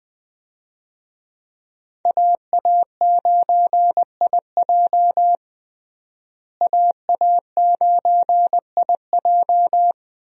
Morse code, heard as a keyed tone, about 21 wpm, 700 Hz. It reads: AA9IJ AA9IJ
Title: USA Call Signs - 5 Characters - Repeated 2x - 20wpm